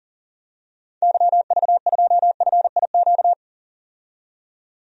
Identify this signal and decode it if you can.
Morse code, heard as a keyed tone, about 37 wpm, 700 Hz. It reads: YV2FIQ
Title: DXCC-100 Call Signs - 6 Characters - 40wpm